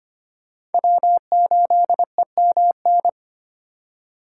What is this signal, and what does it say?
Morse code, heard as a keyed tone, about 31 wpm, 700 Hz. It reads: W8EMN